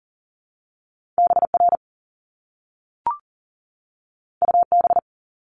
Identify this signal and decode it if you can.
Morse code, heard as a keyed tone, about 38 wpm, 700 Hz. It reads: BR UB
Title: Top Two Letter Combinations in Words 40wpm